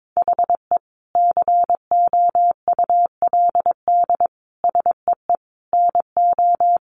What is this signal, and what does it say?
Morse code, heard as a keyed tone, about 22 wpm, 700 Hz. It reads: HE COULD SEE NO
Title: Sentences from Top 1200 Words 22wpm